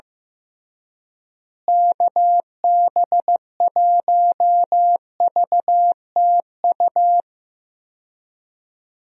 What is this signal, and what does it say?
Morse code, heard as a keyed tone, about 15 wpm, 700 Hz. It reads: KB1VTU
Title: USA Call Signs - 6 Characters - Repeated 3x - 15wpm